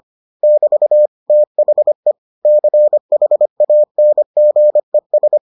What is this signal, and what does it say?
Morse code, heard as a keyed tone, about 25 wpm, 600 Hz. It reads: <BT> THE CHANGES